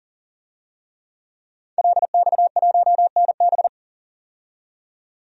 Morse code, heard as a keyed tone, about 40 wpm, 700 Hz. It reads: LX1NB
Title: DXCC-100 Call Signs - 5 Characters - 40wpm